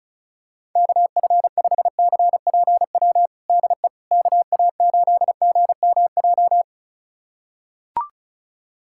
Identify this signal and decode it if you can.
Morse code, heard as a keyed tone, about 34 wpm, 700 Hz. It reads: KF5CPW DE KA8GMJ
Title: USA Callsign DE Callsign - 6 Characters - 35wpm